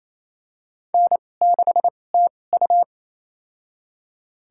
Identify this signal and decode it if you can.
Morse code, heard as a keyed tone, about 27 wpm, spaced as for 20 wpm, 700 Hz. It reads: N6TU
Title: USA Call Signs - 4 Characters - 2x Character Spacing - 28wpm